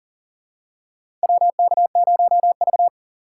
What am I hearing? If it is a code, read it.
Morse code, heard as a keyed tone, about 40 wpm, 700 Hz. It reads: WK0V
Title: USA Call Signs - 2x Character Spacing - 40wpm